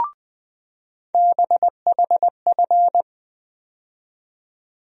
Morse code, heard as a keyed tone, about 20 wpm, 700 Hz. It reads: BHF
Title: USA Call Sign Postfixes - 3 Letters 20wpm